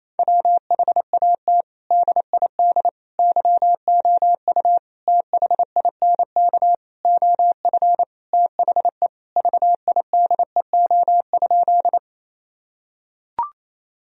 Morse code, heard as a keyed tone, about 28 wpm, 700 Hz. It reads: WHAT DID YOU THINK OF THE VIDEO?
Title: Sentences from Top 1000 Words - Review Part 4 - 28wpm